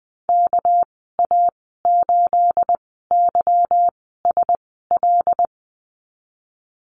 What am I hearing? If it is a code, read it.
Morse code, heard as a keyed tone, about 20 wpm, spaced as for 14 wpm, 700 Hz. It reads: KA8YSL
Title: 219 Lesson 35g - Number 0 - Call Signs 20wpm